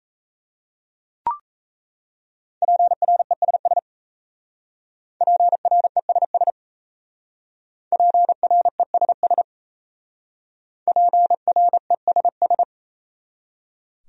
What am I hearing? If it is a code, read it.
Morse code, heard as a keyed tone, about 35 wpm, 700 Hz. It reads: PRESS PRESS PRESS PRESS
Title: Speed-Racing Top 300 Words 28wpm